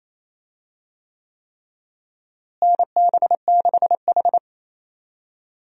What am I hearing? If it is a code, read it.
Morse code, heard as a keyed tone, about 27 wpm, 700 Hz. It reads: NB6H
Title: USA Call Signs - 4 Characters - Repeated 3x - 28wpm